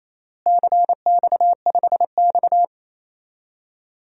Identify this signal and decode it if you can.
Morse code, heard as a keyed tone, about 27 wpm, 700 Hz. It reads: CX5X
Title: DXCC-100 Call Signs - 4 Characters - Repeated 2x - 28wpm